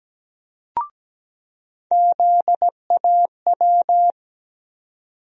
Morse code, heard as a keyed tone, about 17 wpm, 700 Hz. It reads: ZAW
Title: USA Call Sign Postfixes - 3 Letters 17wpm